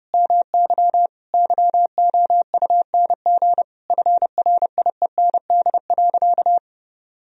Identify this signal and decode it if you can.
Morse code, heard as a keyed tone, about 32 wpm, 700 Hz. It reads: MY YOUNG FRIEND.
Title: Sentences from Top 1000 Words - Review Part 3 - 30wpm